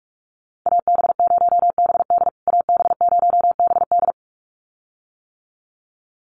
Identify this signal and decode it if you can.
Morse code, heard as a keyed tone, about 44 wpm, 700 Hz. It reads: AB0BD AB0BD
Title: USA Call Signs Repeated 2x 45wpm